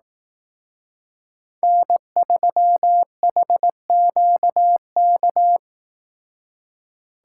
Morse code, heard as a keyed tone, about 18 wpm, 700 Hz. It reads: N3HQK
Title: Speed Racing - USA 1x3 Call Signs 15wpm